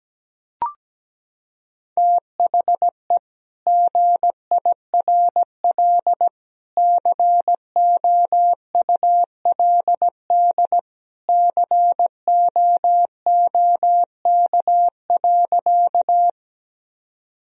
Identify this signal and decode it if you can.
Morse code, heard as a keyed tone, about 17 wpm, 700 Hz. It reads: THE GIRL COULD COOK.
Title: Sentences from Top 1000 Words - Review Part 4 - 17wpm